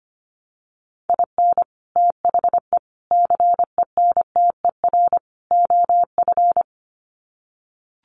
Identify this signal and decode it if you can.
Morse code, heard as a keyed tone, about 25 wpm, 700 Hz. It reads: IN THE CENTER OF
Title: Sets of 4 Words 25wpm